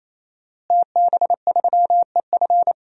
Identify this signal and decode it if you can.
Morse code, heard as a keyed tone, about 28 wpm, 700 Hz. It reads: TB3EF
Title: DXCC-100 Call Signs - 5 Characters - Repeated 3x - 28wpm